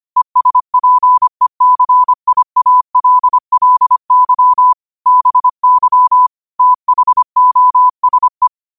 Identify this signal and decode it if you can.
Morse code, heard as a keyed tone, about 25 wpm, 1000 Hz. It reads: ESPECIALLY BY THOSE